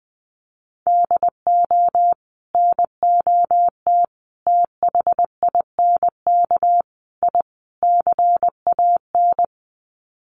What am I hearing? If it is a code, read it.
Morse code, heard as a keyed tone, about 20 wpm, 700 Hz. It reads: DO NOT THINK I CAN
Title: Sets of 5 Words Encore2 20wpm